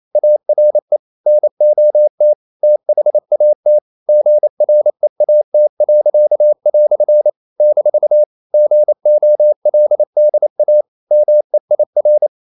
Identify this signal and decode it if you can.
Morse code, heard as a keyed tone, about 28 wpm, 600 Hz. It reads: ARE NOT THAT GREAT." - GOLDA MEIR
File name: CW / Practice / EnglishQuotations0128